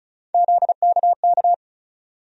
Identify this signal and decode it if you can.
Morse code, heard as a keyed tone, about 36 wpm, 700 Hz. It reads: ZKK
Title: USA Call Sign Postfixes - 3 Letters 35wpm